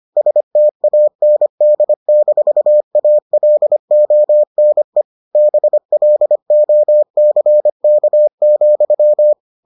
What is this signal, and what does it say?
Morse code, heard as a keyed tone, about 25 wpm, 600 Hz. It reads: STAND-ALONE BLOCK,